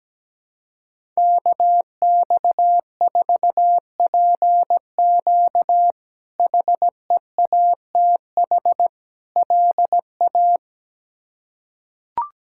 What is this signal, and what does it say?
Morse code, heard as a keyed tone, about 17 wpm, 700 Hz. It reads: KX4PQ HEATH LA
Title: K1USN SST QSO Practice - 17wpm